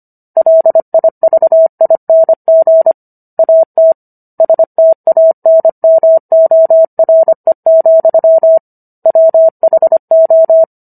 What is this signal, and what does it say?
Morse code, heard as a keyed tone, about 25 wpm, 650 Hz. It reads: LIVING AT STANMORE, WHO